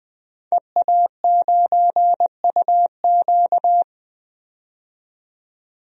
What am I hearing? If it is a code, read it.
Morse code, heard as a keyed tone, about 21 wpm, 700 Hz. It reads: EA9UQ
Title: DXCC-100 Call Signs - 5 Characters - 20wpm